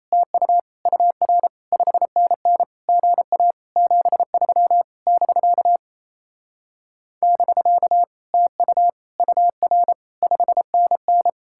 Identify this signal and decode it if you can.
Morse code, heard as a keyed tone, about 37 wpm, 700 Hz. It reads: TU UR 5NN GA 73 <BK> <BK> TU UR 5NN